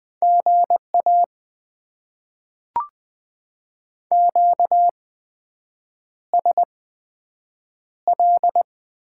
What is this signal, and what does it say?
Morse code, heard as a keyed tone, about 20 wpm, spaced as for 7 wpm, 700 Hz. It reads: GA QSL
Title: Top 90 QSO Elements - 8x Character Spacing - 20wpm